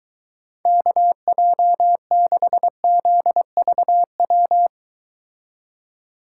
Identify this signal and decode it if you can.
Morse code, heard as a keyed tone, about 22 wpm, 700 Hz. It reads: KJ6ZVW